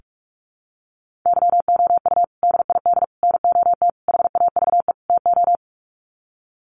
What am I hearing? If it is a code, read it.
Morse code, heard as a keyed tone, about 44 wpm, 700 Hz. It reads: YOU DID NOT HAVE TO
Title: Sets of 5 Words Encore 45wpm